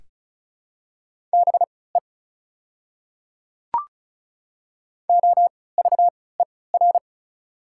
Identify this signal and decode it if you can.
Morse code, heard as a keyed tone, about 36 wpm, spaced as for 20 wpm, 690 Hz. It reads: BE OVER
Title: 3x Character Spacing - Top 100 Words - 35wpm